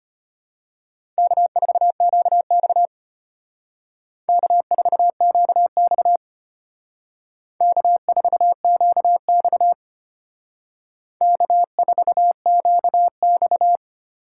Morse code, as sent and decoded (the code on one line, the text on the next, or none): -.- ....- --.- -..- / -.- ....- --.- -..- / -.- ....- --.- -..- / -.- ....- --.- -..-
K4QX K4QX K4QX K4QX